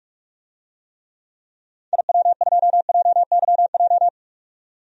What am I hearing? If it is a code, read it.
Morse code, heard as a keyed tone, about 47 wpm, 700 Hz. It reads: IW2JYJ
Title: DXCC-100 Call Signs - 6 Characters - 45wpm